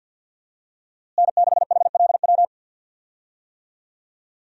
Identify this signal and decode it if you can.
Morse code, heard as a keyed tone, about 50 wpm, 700 Hz. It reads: N6HLW
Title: USA Call Signs - 5 Characters - Repeated 3x - 50wpm